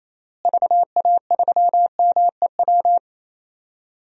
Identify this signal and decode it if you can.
Morse code, heard as a keyed tone, about 28 wpm, 700 Hz. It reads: VA3MEW
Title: DXCC-100 Call Signs - 6 Characters - Repeated 3x - 28wpm